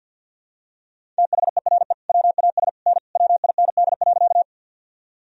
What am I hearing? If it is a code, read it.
Morse code, heard as a keyed tone, about 51 wpm, 700 Hz. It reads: THERE WAS N WIND.
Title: Sentences from Top 400 Words and Abbreviations 50wpm